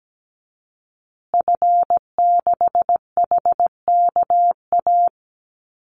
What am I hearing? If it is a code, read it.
Morse code, heard as a keyed tone, about 17 wpm, 700 Hz. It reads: F6HKA
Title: Call Signs Repeated 2x - 17wpm